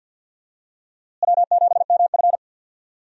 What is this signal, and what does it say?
Morse code, heard as a keyed tone, about 52 wpm, 700 Hz. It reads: W7MF
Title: USA Call Signs - 4 Characters - Repeated 2x - 50wpm